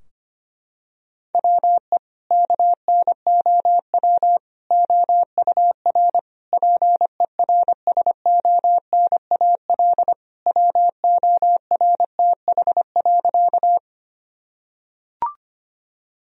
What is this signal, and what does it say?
Morse code, heard as a keyed tone, about 25 wpm, 700 Hz. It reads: WE KNOW OUR PERSONAL WORTH.
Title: Sentences from Top 1000 Words - Review Part 4 - 25wpm